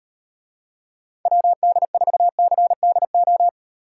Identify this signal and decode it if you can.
Morse code, heard as a keyed tone, about 37 wpm, 700 Hz. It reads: WD4CDO